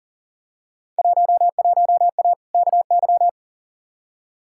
Morse code, heard as a keyed tone, about 39 wpm, 700 Hz. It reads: 11A KY
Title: Field Day Exchanges Encore - 40wpm